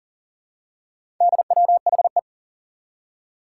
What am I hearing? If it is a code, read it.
Morse code, heard as a keyed tone, about 39 wpm, 700 Hz. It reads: DWHE
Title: Four Challenging Letter-Number Combinations 40wpm